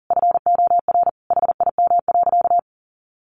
Morse code, heard as a keyed tone, about 40 wpm, 700 Hz. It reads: FOR HIM.
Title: Sentences from Top 500 Words 40wpm